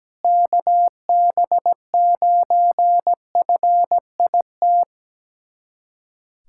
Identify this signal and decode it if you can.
Morse code, heard as a keyed tone, about 17 wpm, 700 Hz. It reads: KB9FIT
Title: USA Callsign DE Callsign 17wpm